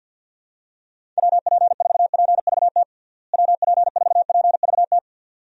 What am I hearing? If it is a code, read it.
Morse code, heard as a keyed tone, about 48 wpm, 700 Hz. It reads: WP4PVT WP4PVT
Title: USA Call Signs - 6 Characters - Repeated 2x - 50wpm